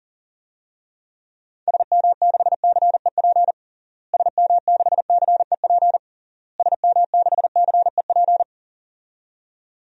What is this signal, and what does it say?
Morse code, heard as a keyed tone, about 40 wpm, 700 Hz. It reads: SM6CEP SM6CEP SM6CEP
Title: DXCC-100 Call Signs - 6 Characters - Repeated 3x - 40wpm